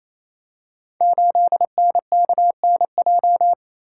Morse code, heard as a keyed tone, about 27 wpm, 700 Hz. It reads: 8NKNJ
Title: Code Groups - 5 Letter:Numbers - 28wpm